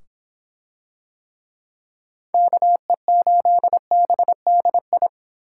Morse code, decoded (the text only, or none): KE8BDI